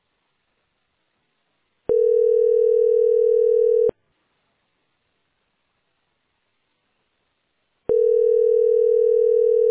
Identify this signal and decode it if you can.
Ringback tone (North America), 440 + 480 Hz.